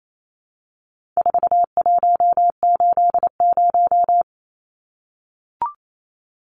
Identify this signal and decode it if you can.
Morse code, heard as a keyed tone, about 28 wpm, 700 Hz. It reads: 4180
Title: Numbers - 4-Digit 28wpm